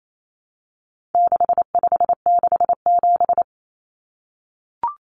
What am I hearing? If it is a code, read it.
Morse code, heard as a keyed tone, about 28 wpm, 700 Hz. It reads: 6567
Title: Numbers - 4-Digit 28wpm